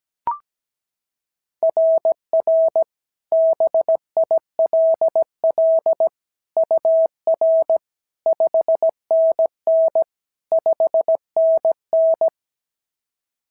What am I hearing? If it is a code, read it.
Morse code, heard as a keyed tone, about 16 wpm, 650 Hz. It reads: RR BILL UR 5NN 5NN
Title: SOTA QSO Practice 17wpm